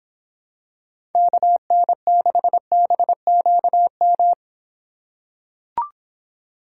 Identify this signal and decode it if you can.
Morse code, heard as a keyed tone, about 26 wpm, 700 Hz. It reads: KN6BQM